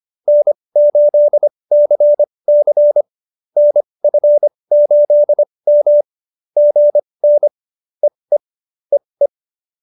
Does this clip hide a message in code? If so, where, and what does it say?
Yes: Morse code from 0.3 to 9.3 s, heard as a keyed tone, about 25 wpm, spaced as for 20 wpm, 590 Hz. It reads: N8CC NF8M GN EE EE